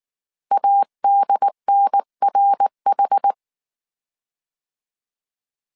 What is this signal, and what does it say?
Morse code, heard as a keyed tone, about 19 wpm, 800 Hz. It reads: ADNRH